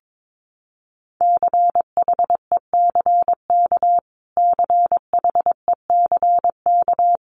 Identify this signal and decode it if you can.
Morse code, heard as a keyed tone, about 22 wpm, 700 Hz. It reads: CHECK CHECK
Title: Top 500 Words Repeated 22wpm